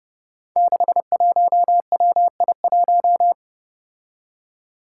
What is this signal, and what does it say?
Morse code, heard as a keyed tone, about 30 wpm, 700 Hz. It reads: 61WI1
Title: Five Challenging Letter-Number Combinations 30wpm